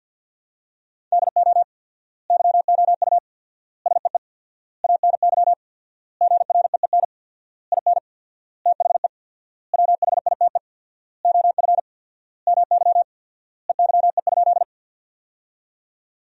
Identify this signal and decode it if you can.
Morse code, heard as a keyed tone, about 51 wpm, 700 Hz. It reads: DO YOU SEE ANY GREEN IN THE WHITE OF MY EYE?
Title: 4x Word-Spacing - Sentences from Top 500 Words - 50wpm